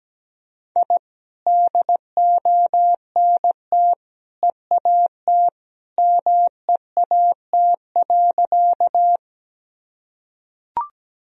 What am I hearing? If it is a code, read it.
Morse code, heard as a keyed tone, about 17 wpm, 700 Hz. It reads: I DONT EAT MEAT.
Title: Sentences from Top 1000 Words - Review Part 3 - 17wpm